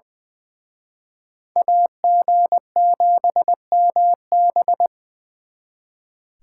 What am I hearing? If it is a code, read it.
Morse code, heard as a keyed tone, about 25 wpm, 700 Hz. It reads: AG7MB
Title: Speed Racing - USA 2x2 Call Signs 20wpm